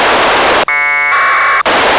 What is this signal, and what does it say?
Packet radio (AX.25, as in APRS).